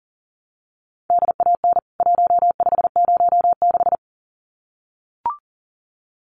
Morse code, heard as a keyed tone, about 40 wpm, 700 Hz. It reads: DAN 1506
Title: CWT Exchange 40wpm